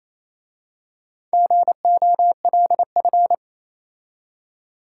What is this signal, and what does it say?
Morse code, heard as a keyed tone, about 27 wpm, 690 Hz. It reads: GOLF
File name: Easy Words - Speed-Racing with Hints - 22wpm